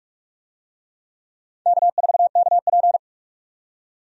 Morse code, heard as a keyed tone, about 46 wpm, 700 Hz. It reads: K4KP
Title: USA Call Signs - 4 Characters - 45wpm